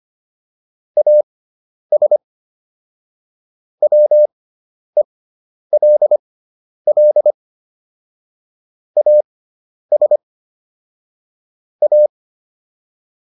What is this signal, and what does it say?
Morse code, heard as a keyed tone, about 25 wpm, spaced as for 10 wpm, 600 Hz. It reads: AS WELL AS A